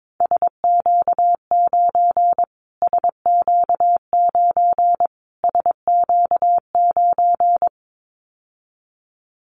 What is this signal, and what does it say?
Morse code, heard as a keyed tone, about 22 wpm, 700 Hz. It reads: SQ9 SQ9 SQ9
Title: Callsign Prefixes for Top 25 DXCC - Repeated 3x - 22wpm